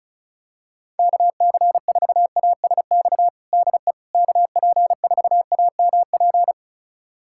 Morse code, heard as a keyed tone, about 36 wpm, 700 Hz. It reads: KC4ASX DE KP4AMP
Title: USA Callsign DE Callsign 35wpm